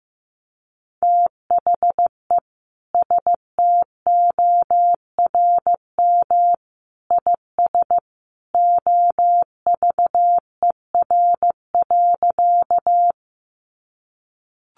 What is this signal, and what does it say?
Morse code, heard as a keyed tone, about 15 wpm, 700 Hz. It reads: THE STORM IS OVER.
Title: Sentences from Top 1500 Words 15wpm